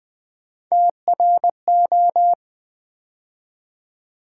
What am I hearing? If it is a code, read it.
Morse code, heard as a keyed tone, about 20 wpm, 700 Hz. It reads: TRO
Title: Build A Word - 3 Letters At A Time - 8-Letter Words - 20wpm